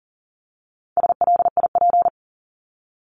Morse code, heard as a keyed tone, about 40 wpm, 700 Hz. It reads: SLIP
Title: Top 1751-2000 Words 40wpm